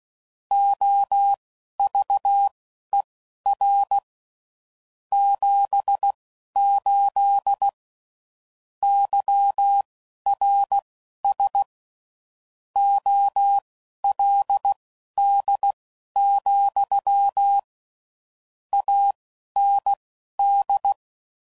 Morse code, heard as a keyed tone, about 16 wpm, spaced as for 11 wpm, 800 Hz. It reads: OVER 78 YRS OLD, AND
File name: CW Code Groups @10WPM / QSOs / QSO kc7haa